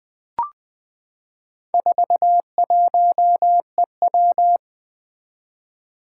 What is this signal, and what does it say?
Morse code, heard as a keyed tone, about 21 wpm, 700 Hz. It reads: 41EW
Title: Four Challenging Letter-Number Combinations 20wpm